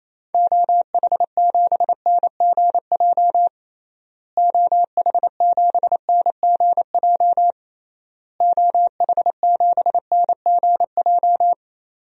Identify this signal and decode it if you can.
Morse code, heard as a keyed tone, about 28 wpm, 700 Hz. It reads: OH7NGJ OH7NGJ OH7NGJ
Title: DXCC-100 Call Signs - 6 Characters - Repeated 3x - 28wpm